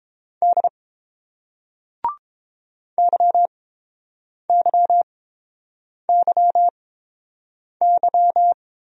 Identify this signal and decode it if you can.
Morse code, heard as a keyed tone, about 31 wpm, spaced as for 7 wpm, 700 Hz. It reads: D YYYY